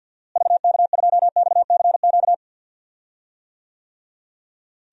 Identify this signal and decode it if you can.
Morse code, heard as a keyed tone, about 51 wpm, 700 Hz. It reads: VK2XCQ
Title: DXCC-100 Call Signs - 6 Characters - Repeated 3x - 50wpm